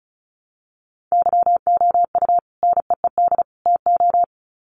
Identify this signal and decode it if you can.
Morse code, heard as a keyed tone, about 34 wpm, 700 Hz. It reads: YOU NEED TO
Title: Sets of 3 Words 35wpm